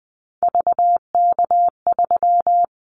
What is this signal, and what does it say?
Morse code, heard as a keyed tone, about 20 wpm, 700 Hz. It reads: VK3
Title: Callsign Prefixes for Top 100 DXCC - 20wpm